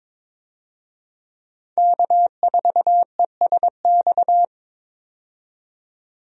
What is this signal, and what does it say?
Morse code, heard as a keyed tone, about 22 wpm, 700 Hz. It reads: K4ESX
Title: USA Call Signs - 5 Characters - 22wpm